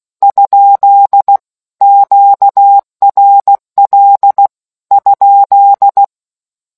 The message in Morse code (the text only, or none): ? QRL ?